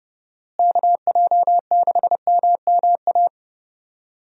Morse code, heard as a keyed tone, about 30 wpm, 700 Hz. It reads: KJ6MMA
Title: USA Call Signs - 6 Characters - Repeated 3x - 30wpm